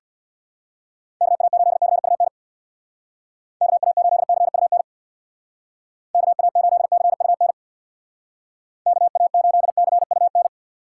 Morse code, heard as a keyed tone, about 61 wpm, 700 Hz. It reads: KA8CUN KA8CUN KA8CUN KA8CUN
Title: Speed Racing - USA 2x3 Call Signs 50wpm